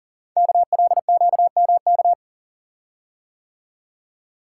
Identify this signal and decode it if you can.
Morse code, heard as a keyed tone, about 39 wpm, 690 Hz. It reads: KLQMK
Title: Code Groups - 5 Letters - 40wpm